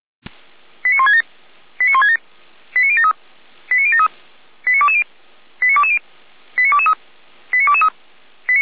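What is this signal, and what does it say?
Five-tone selective call (ZVEI), heard as quick runs of five tones, tones of about 80 ms. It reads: 89167 89167 89073 89073 89220 89220 89223 89223
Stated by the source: Five-tone ZVEI selective call received with an SDR, from the server of the Feuerwehr Orlamünde fire brigade.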